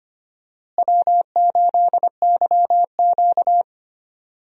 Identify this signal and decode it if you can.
Morse code, heard as a keyed tone, about 25 wpm, 700 Hz. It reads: W8YQ
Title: USA Call Signs - 4 Characters - Repeated 3x - 25wpm